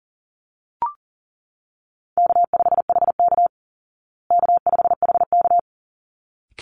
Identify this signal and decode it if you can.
Morse code, heard as a keyed tone, about 40 wpm, 700 Hz. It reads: K5HK K5HK
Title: Easy Call Signs Repeated 2x - 40wpm